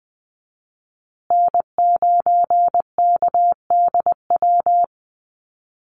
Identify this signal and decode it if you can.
Morse code, heard as a keyed tone, about 20 wpm, 700 Hz. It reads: N9KDW
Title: USA Call Signs Repeated 3x 20wpm